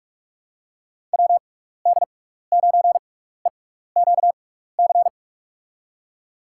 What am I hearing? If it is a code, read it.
Morse code, heard as a keyed tone, about 44 wpm, spaced as for 15 wpm, 700 Hz. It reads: WD9EQC
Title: USA Call Signs - 6x Character Spacing - 45wpm